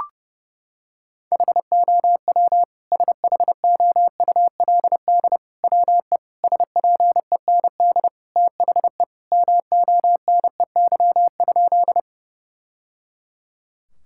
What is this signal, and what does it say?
Morse code, heard as a keyed tone, about 29 wpm, 700 Hz. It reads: HOW SHOULD WE SPEND THE MONEY?
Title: Sentences from Top 1000 Words - Review Part 5 - 30wpm